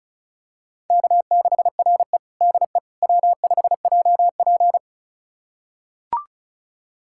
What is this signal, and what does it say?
Morse code, heard as a keyed tone, about 36 wpm, 700 Hz. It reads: K6RE DE W5JP
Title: USA Callsign DE Callsign - 4 Characters - 35wpm